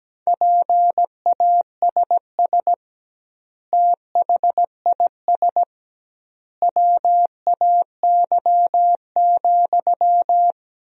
Morse code, heard as a keyed tone, about 17 wpm, 700 Hz. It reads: PASS THIS WAY,
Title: 2x Word-Spacing - Sentences from Top 400 Words - 17wpm